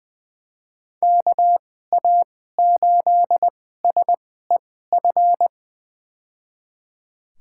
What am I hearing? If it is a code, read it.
Morse code, heard as a keyed tone, about 20 wpm, spaced as for 15 wpm, 700 Hz. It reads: KA8SEF
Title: USA Call Signs - 6 Characters - 2x Character Spacing - 20wpm